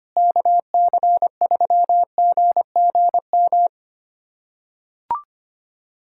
Morse code, heard as a keyed tone, about 25 wpm, 700 Hz. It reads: KC3GGM
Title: USA Callsign DE Callsign - 6 Characters - 25wpm